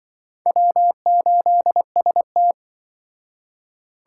Morse code, heard as a keyed tone, about 23 wpm, 700 Hz. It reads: W8ST
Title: Speed Racing - USA Call Signs 20wpm